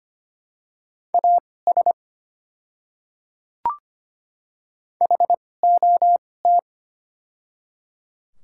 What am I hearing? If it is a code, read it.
Morse code, heard as a keyed tone, about 25 wpm, spaced as for 18 wpm, 690 Hz. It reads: AS HOT
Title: 2x Character Spacing - Top 200 Words - 25wpm